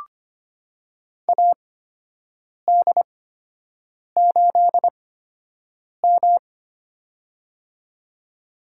Morse code, heard as a keyed tone, about 25 wpm, spaced as for 7 wpm, 700 Hz. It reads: AD8M
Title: USA Call Signs - 4 Characters - 8x Character Spacing - 25wpm